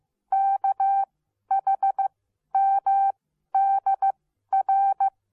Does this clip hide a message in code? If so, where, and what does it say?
Yes: Morse code from 0.3 to 5.1 s, heard as a keyed tone, about 15 wpm, spaced as for 11 wpm, 780 Hz. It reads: KHMDR